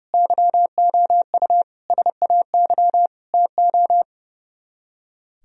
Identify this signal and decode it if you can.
Morse code, heard as a keyed tone, about 36 wpm, 700 Hz. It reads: YOU SAY TO
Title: Speed-Racing - Sets of 5 Words 30wpm